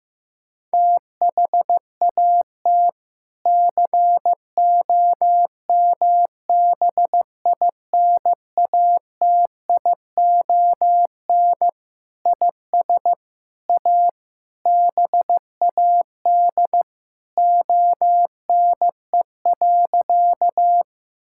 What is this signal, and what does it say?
Morse code, heard as a keyed tone, about 15 wpm, 700 Hz. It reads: THAT COMBINATION IS A BAD ONE.
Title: Sentences from Top 1000 Words - Review Part 6 - 15wpm